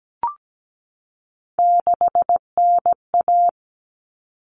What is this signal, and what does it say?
Morse code, heard as a keyed tone, about 17 wpm, 700 Hz. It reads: 6NA
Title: Common Cut Numbers - 3-Digit 17wpm